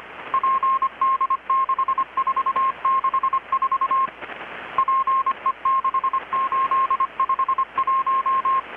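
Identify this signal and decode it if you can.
Morse code, heard as a keyed tone, about 25 wpm, 1070 Hz. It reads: PD6464 PE6851
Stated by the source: Morse from KPH's Night of Nights XX maritime radiotelegraphy event, received in Chiapas, Mexico about 3,800 km away on a Kenwood R-600 with a grounded 10 m long-wire antenna.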